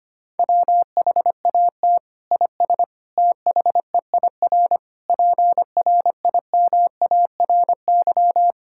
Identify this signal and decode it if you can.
Morse code, heard as a keyed tone, about 25 wpm, 700 Hz. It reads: WHAT IS THEIR PRIMARY
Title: Sentences from Top 1000 Words - Review Part 5 - 25wpm